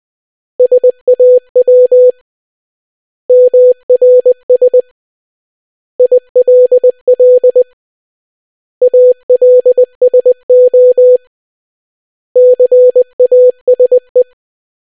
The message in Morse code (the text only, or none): SAW MRS ILL ALSO CASE